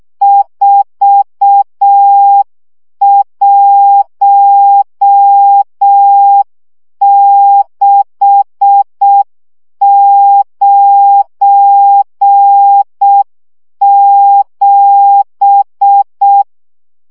Morse code, heard as a keyed tone, about 6 wpm, 800 Hz. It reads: 41697
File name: morze vizsga 2011-3 5